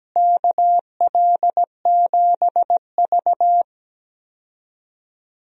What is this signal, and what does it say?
Morse code, heard as a keyed tone, about 17 wpm, 700 Hz. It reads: KL7V
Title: DXCC-100 Call Signs - 4 Characters - Repeated 2x - 17wpm